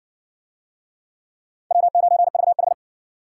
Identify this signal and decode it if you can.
Morse code, heard as a keyed tone, about 60 wpm, 700 Hz. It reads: W9VH